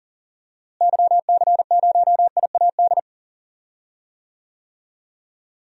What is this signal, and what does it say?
Morse code, heard as a keyed tone, about 39 wpm, 700 Hz. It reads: YC0IAD
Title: DXCC-100 Call Signs - 6 Characters - Repeated 2x - 40wpm